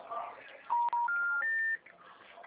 Special information tone.
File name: out- -2017- - - .25943